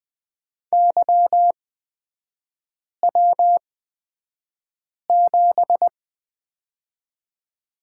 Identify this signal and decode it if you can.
Morse code, heard as a keyed tone, about 20 wpm, spaced as for 5 wpm, 700 Hz. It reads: YW7
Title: 224 Lesson 36e - BT - Three Characters 20wpm